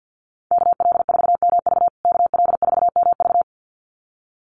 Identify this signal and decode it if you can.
Morse code, heard as a keyed tone, about 50 wpm, 700 Hz. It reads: KL4MV KL4MV
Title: USA Call Signs Repeated 2x 50wpm